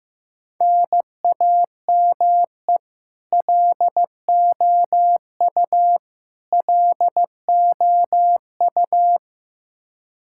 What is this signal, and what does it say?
Morse code, heard as a keyed tone, about 15 wpm, 700 Hz. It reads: NAME LOU LOU
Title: Practice QSO - Exchange 1 - Beginning - 15wpm